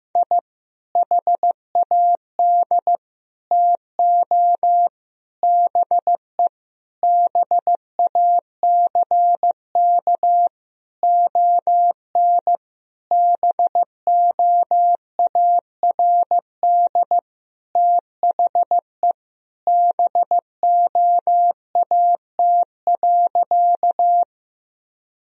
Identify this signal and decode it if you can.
Morse code, heard as a keyed tone, about 15 wpm, 700 Hz. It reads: I HAD TO BE BACK ON BOARD THE BOAT.
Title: Sentences from Top 1000 Words - Review Part 4 - 15wpm